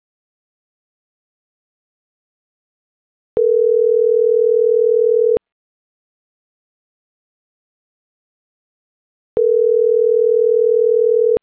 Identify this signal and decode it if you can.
Ringback tone (North America), 440 + 480 Hz.